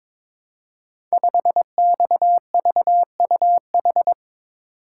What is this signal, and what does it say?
Morse code, heard as a keyed tone, about 22 wpm, 700 Hz. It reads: 5XVUH
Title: Code Groups - 5 Letter:Numbers - 22wpm